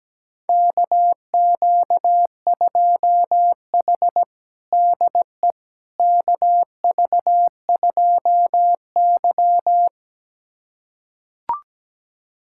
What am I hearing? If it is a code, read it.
Morse code, heard as a keyed tone, about 17 wpm, 700 Hz. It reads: KQ2H DE KV2Y
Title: USA Callsign DE Callsign - 4 Characters - 17wpm